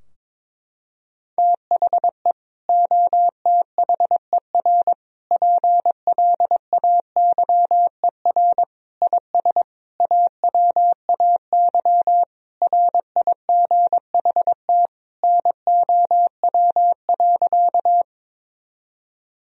Morse code, heard as a keyed tone, about 22 wpm, 700 Hz. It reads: THE OTHER PLAYER IS AWAY RIGHT NOW.
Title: Sentences from Top 1000 Words - Review Part 4 - 22wpm